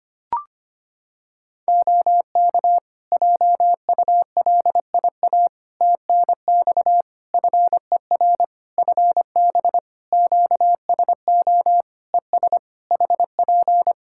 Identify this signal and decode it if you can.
Morse code, heard as a keyed tone, about 25 wpm, 700 Hz. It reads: OK JULIA TNX FER FB QSO ES HP
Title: Practice QSO - Exchange 4 - Ending - 25wpm